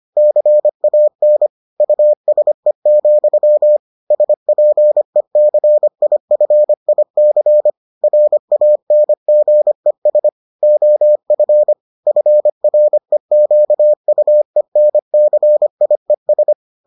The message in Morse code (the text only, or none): CAN USE, SPECIFIC RANGES OF FREQUENCIES